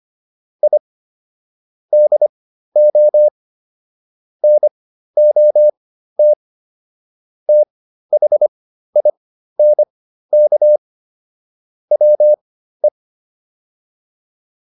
Morse code, heard as a keyed tone, about 25 wpm, spaced as for 13 wpm, 600 Hz. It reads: I DO NOT THINK WE